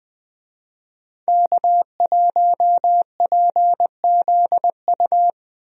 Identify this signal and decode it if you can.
Morse code, heard as a keyed tone, about 21 wpm, 700 Hz. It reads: K1PZU
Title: USA Call Signs - 5 Characters - Repeated 2x - 20wpm